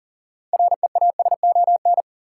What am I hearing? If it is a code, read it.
Morse code, heard as a keyed tone, about 40 wpm, 700 Hz. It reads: REASON
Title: Sentences from Top 700 Words - New Word Drill - 40wpm